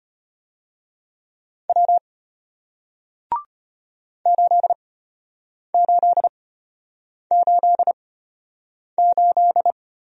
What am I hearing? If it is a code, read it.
Morse code, heard as a keyed tone, about 34 wpm, spaced as for 8 wpm, 700 Hz. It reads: W 8888